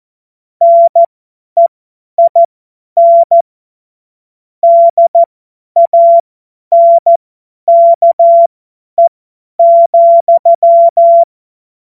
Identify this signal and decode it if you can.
Morse code, heard as a keyed tone, about 14 wpm, spaced as for 10 wpm, 680 Hz. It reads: NEIN DANKE,
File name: Morsekurs / Witze / Witz 25. Januar 2022, 50 F70